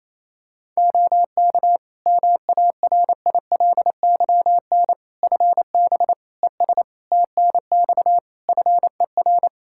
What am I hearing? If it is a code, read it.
Morse code, heard as a keyed tone, about 28 wpm, 700 Hz. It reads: OK MARILYN FB ES TNX FER
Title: Practice QSO - Exchange 2 - Info - 28wpm